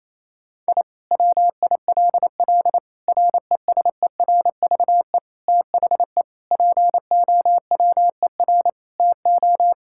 Morse code, heard as a keyed tone, about 27 wpm, 700 Hz. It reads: I WILL RESERVE THE POWER TO
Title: Sentences from Top 1000 Words - Review Part 6 - 28wpm